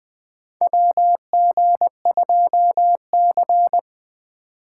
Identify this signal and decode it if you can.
Morse code, heard as a keyed tone, about 21 wpm, 700 Hz. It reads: WG2C
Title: USA Call Signs - 4 Characters - 4x Character Spacing - 20wpm